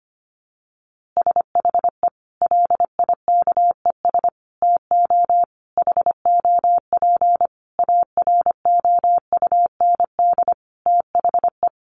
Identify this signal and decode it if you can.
Morse code, heard as a keyed tone, about 25 wpm, 700 Hz. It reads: SHE LIKES TO HOP AROUND THE
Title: Easy Phrases 25wpm